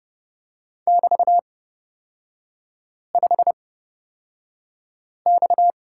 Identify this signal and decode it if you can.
Morse code, heard as a keyed tone, about 30 wpm, spaced as for 5 wpm, 690 Hz. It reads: <BT>5X
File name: Lesson 37e - Letter x - Three Characters 30wpm